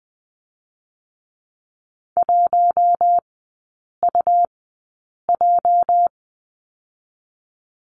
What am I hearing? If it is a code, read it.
Morse code, heard as a keyed tone, about 20 wpm, 700 Hz. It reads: 1 U J
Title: 205 Lesson 33e - Letter j - Three Characters 20wpm